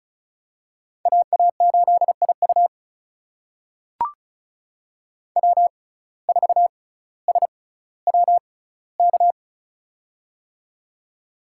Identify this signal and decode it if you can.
Morse code, heard as a keyed tone, about 36 wpm, spaced as for 18 wpm, 700 Hz. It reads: AA8IU W4SWK
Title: USA Call Signs - 5 Characters - 6x Character Spacing - 35wpm